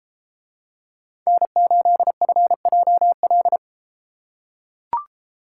Morse code, heard as a keyed tone, about 33 wpm, 700 Hz. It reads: N8FJL